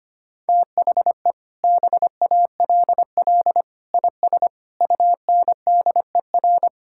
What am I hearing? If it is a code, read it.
Morse code, heard as a keyed tone, about 25 wpm, 700 Hz. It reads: THE BALL IS UNDER
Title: Sentences from Top 1000 Words - Review Part 6 - 25wpm